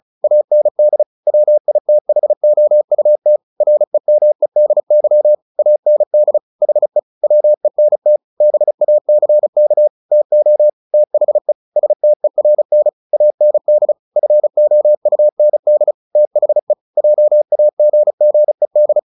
Morse code, heard as a keyed tone, about 34 wpm, 600 Hz. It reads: AND WITHOUT REMEDY AND HE WENT BACK TO THE STERN AND FOUND THE JAGGED
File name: the old man and the sea 2 35wpm